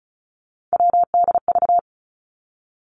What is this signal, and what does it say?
Morse code, heard as a keyed tone, about 36 wpm, 700 Hz. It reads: WDV
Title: Three Challenging Letter-Number Combinations 35wpm